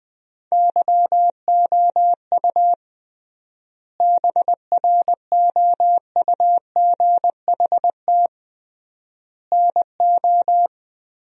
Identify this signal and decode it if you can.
Morse code, heard as a keyed tone, about 20 wpm, 700 Hz. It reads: YOU BROUGHT NO
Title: 3x Word-Spacing - Sentences from Top 500 Words - 20wpm